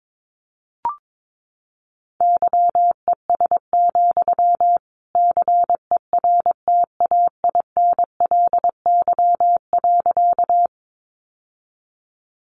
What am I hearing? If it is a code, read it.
Morse code, heard as a keyed tone, about 22 wpm, 700 Hz. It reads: YES, CERTAINLY.
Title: Sentences from Top 900 Words 22wpm